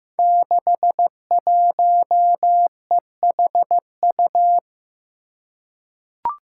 Morse code, heard as a keyed tone, about 15 wpm, 700 Hz. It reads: 61EHU
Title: Five Challenging Letter-Number Combinations 15wpm